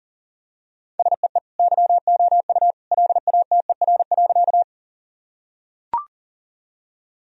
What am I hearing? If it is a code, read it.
Morse code, heard as a keyed tone, about 40 wpm, 700 Hz. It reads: SEE YOU LATER.
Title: Sentences from Top 1000 Words - Review Part 4 - 40wpm